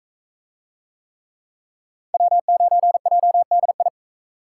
Morse code, heard as a keyed tone, about 39 wpm, 700 Hz. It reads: W9JDI